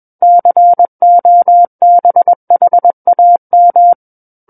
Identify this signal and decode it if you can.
Morse code, heard as a keyed tone, about 21 wpm, 700 Hz. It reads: COBHAM
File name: Book One / 12-Chapter 12 - 21 WPM